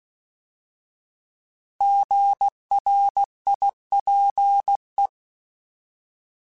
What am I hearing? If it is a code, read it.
Morse code, heard as a keyed tone, about 16 wpm, 800 Hz. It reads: GRIPE